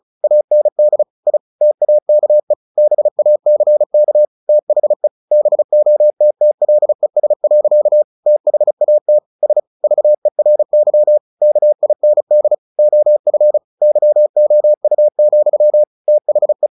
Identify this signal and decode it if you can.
Morse code, heard as a keyed tone, about 34 wpm, 600 Hz. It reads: AND I TAKE BACK THE BOTTLES. THAT S VERY KIND OF YOU, THE
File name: the old man and the sea / the old man and the sea 1 35wpm